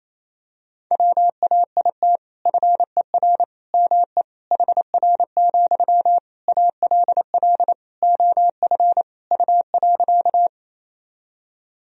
Morse code, heard as a keyed tone, about 28 wpm, 700 Hz. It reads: WAIT FER ME HR, ALL OF U.
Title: Sentences from Top 500 Words and Abbreviations 28wpm